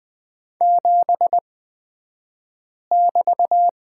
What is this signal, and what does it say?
Morse code, heard as a keyed tone, about 20 wpm, spaced as for 5 wpm, 700 Hz. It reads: 7<BT>
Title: 223 Lesson 36d - BT - Two Characters 20wpm